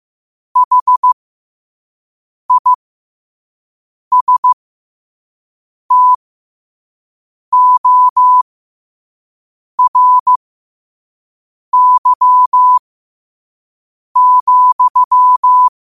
Morse code, heard as a keyed tone, about 15 wpm, spaced as for 5 wpm, 1000 Hz. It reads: HISTORY,
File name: qotdfiles / 5wpm